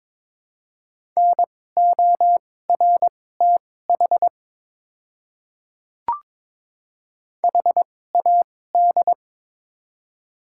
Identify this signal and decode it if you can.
Morse code, heard as a keyed tone, about 22 wpm, spaced as for 16 wpm, 700 Hz. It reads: NORTH HAD
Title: 2x Character Spacing - Top 500 Words - 22wpm